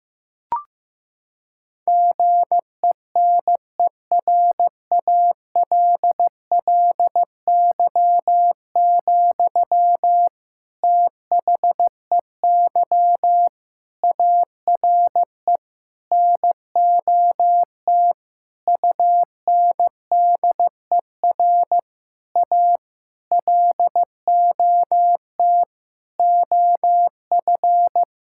Morse code, heard as a keyed tone, about 15 wpm, 700 Hz. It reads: GENERALLY, THEY ARE NOT UNDER A LOT OF
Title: Sentences from Top 1000 Words - Review Part 6 - 15wpm